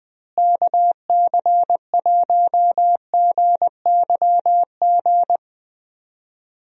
Morse code, heard as a keyed tone, about 20 wpm, 700 Hz. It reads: KC1GYG
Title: USA Call Signs - 6 Characters - Repeated 2x - 20wpm